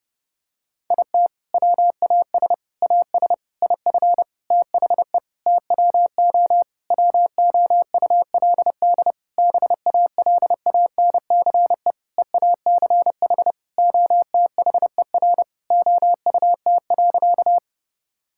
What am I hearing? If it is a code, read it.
Morse code, heard as a keyed tone, about 30 wpm, 700 Hz. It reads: IT WAS AS IF THE TWO WOULD BALANCE EACH OTHER OUT.
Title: Sentences from Top 1000 Words - Review Part 6 - 30wpm